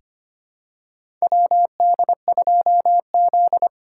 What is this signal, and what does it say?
Morse code, heard as a keyed tone, about 25 wpm, 700 Hz. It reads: WD2Z
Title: USA Call Signs - 2x Character Spacing - 25wpm